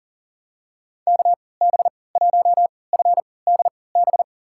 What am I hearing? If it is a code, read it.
Morse code, heard as a keyed tone, about 37 wpm, spaced as for 22 wpm, 700 Hz. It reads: KB1FDB
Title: USA Call Signs - 6 Characters - 3x Character Spacing - 40wpm